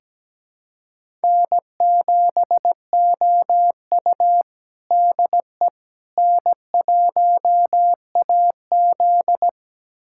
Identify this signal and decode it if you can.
Morse code, heard as a keyed tone, about 17 wpm, 700 Hz. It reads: N7OU DE N1AZ
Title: USA Callsign DE Callsign - 4 Characters - 17wpm